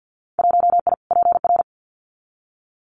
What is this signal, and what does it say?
Morse code, heard as a keyed tone, about 52 wpm, 700 Hz. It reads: 1I PR
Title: Winter Field Day Exchanges Encore 50wpm